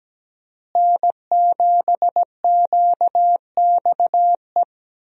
Morse code, heard as a keyed tone, about 17 wpm, 700 Hz. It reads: N7QXE
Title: USA Call Signs - 5 Characters - Repeated 2x - 17wpm